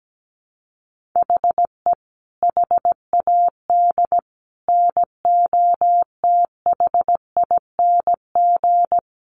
Morse code, heard as a keyed tone, about 17 wpm, 700 Hz. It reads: HE HAD NOTHING
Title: Sentences from Top 1300 Words 17wpm